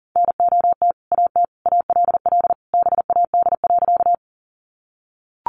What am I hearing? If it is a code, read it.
Morse code, heard as a keyed tone, about 40 wpm, 700 Hz. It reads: NOT AT ALL BAD.
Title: Sentences from Top 700 Words 40wpm